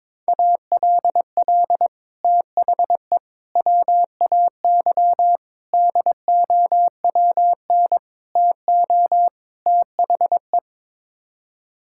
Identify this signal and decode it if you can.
Morse code, heard as a keyed tone, about 22 wpm, 700 Hz. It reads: ALL THE WAY DOWN TO THE
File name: practice / 22wpm / Sets of 6 Words - 22wpm